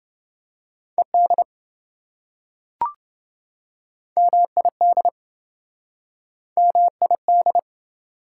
Morse code, heard as a keyed tone, about 27 wpm, 700 Hz. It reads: ED MID MID